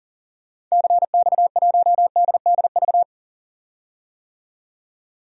Morse code, heard as a keyed tone, about 40 wpm, 700 Hz. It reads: CX1DDV
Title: DXCC-100 Call Signs - 6 Characters - 40wpm